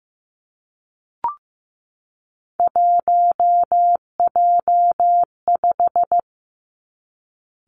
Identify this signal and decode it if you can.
Morse code, heard as a keyed tone, about 15 wpm, 700 Hz. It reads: 1J5
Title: Three Challenging Letter-Number Combinations 15wpm